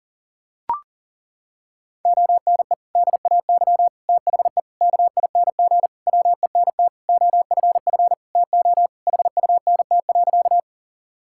Morse code, heard as a keyed tone, about 37 wpm, 700 Hz. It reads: ONE DAY THE KING WENT OFF TO HUNT.
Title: Sentences from Top 1600 Words 40wpm